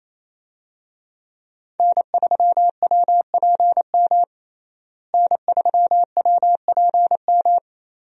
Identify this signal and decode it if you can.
Morse code, heard as a keyed tone, about 27 wpm, 700 Hz. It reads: N3WPM N3WPM
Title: USA Call Signs - 5 Characters - Repeated 3x - 28wpm